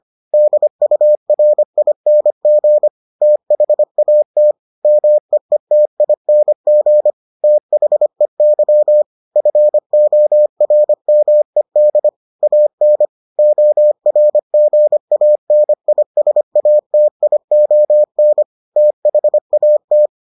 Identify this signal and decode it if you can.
Morse code, heard as a keyed tone, about 25 wpm, 600 Hz. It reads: DURING THAT MEETING THEY FORMED AN ORGANISATION THAT